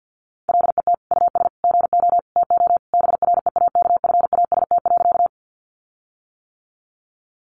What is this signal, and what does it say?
Morse code, heard as a keyed tone, about 51 wpm, 700 Hz. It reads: LET US GO TO BREAKFAST.
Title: Sentences from Top 1500 Words 50wpm